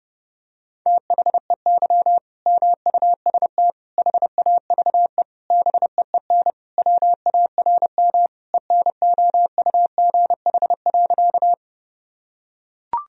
Morse code, heard as a keyed tone, about 30 wpm, 700 Hz. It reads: THEY MUST HAVE BEEN WARM ENOUGH.
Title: Sentences from Top 1000 Words - Review Part 2 - 30wpm